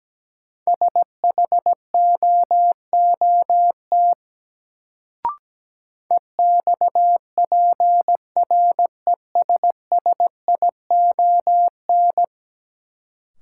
Morse code, heard as a keyed tone, about 17 wpm, 700 Hz. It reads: SHOOT EXPRESSION
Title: Sentences from Top 1200 Words - New Word Drill - 17wpm